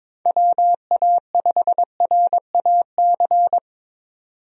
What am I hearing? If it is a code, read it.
Morse code, heard as a keyed tone, about 21 wpm, 700 Hz. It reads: WA5RAC
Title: USA Call Signs - 6 Characters - 22wpm